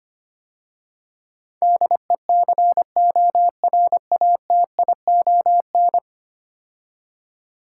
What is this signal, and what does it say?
Morse code, heard as a keyed tone, about 25 wpm, 690 Hz. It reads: DECORATION
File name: 10 Letter Words - First 3 Letters Given - 25wpm